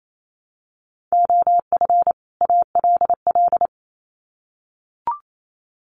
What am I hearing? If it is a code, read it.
Morse code, heard as a keyed tone, about 28 wpm, 700 Hz. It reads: OF ALL
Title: Sets of 2 Words 28wpm